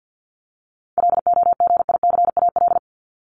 Morse code, heard as a keyed tone, about 48 wpm, 700 Hz. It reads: LOGICAL
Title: Top 1501-1750 Words 50wpm